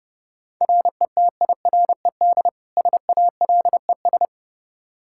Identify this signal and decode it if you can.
Morse code, heard as a keyed tone, about 30 wpm, 700 Hz. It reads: RETIRED SALES
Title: Practice QSO - Exchange 3 - More Info - 30wpm